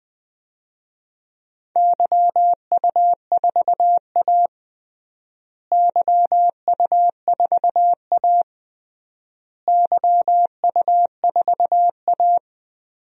Morse code, heard as a keyed tone, about 20 wpm, 700 Hz. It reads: YU4A YU4A YU4A
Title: DXCC-100 Call Signs - 4 Characters - Repeated 3x - 20wpm